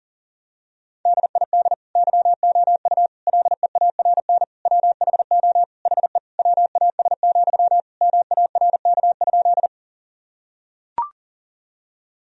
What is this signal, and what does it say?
Morse code, heard as a keyed tone, about 37 wpm, 700 Hz. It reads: DID YOU LEARN WHO HE WAS, MARK?
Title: Sentences from Top 1000 Words - Review Part 1 - 40wpm